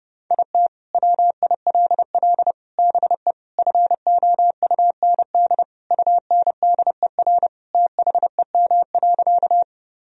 Morse code, heard as a keyed tone, about 30 wpm, 700 Hz. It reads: IT WILL BE FOUND UNDER THEM.
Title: Sentences from Top 1000 Words - Review Part 1 - 30wpm